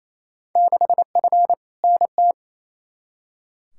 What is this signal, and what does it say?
Morse code, heard as a keyed tone, about 28 wpm, 700 Hz. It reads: 6F NT
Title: Field Day Exchanges Encore - 28wpm